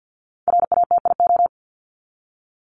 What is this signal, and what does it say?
Morse code, heard as a keyed tone, about 50 wpm, 700 Hz. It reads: RATIO
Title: Top 1751-2000 Words 50wpm